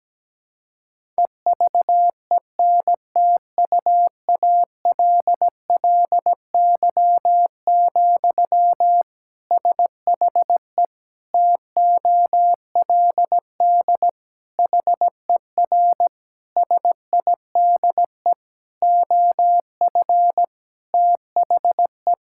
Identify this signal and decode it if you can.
Morse code, heard as a keyed tone, about 17 wpm, 700 Hz. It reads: EVENTUALLY, SHE TOLD HER SIDE OF THE
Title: Sentences from Top 1000 Words - Review Part 6 - 17wpm